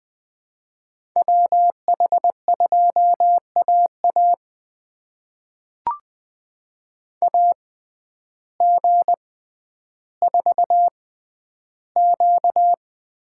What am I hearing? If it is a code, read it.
Morse code, heard as a keyed tone, about 21 wpm, spaced as for 11 wpm, 700 Hz. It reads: WH2AA AG4Q
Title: USA Call Signs - 5 Characters - 6x Character Spacing - 20wpm